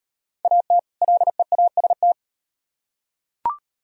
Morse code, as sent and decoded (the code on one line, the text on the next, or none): .- - / .-.. .- ... -
AT LEAST